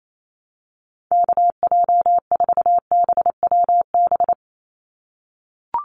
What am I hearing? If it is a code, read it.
Morse code, heard as a keyed tone, about 28 wpm, 700 Hz. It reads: KJ4BWB
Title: USA Call Signs 28wpm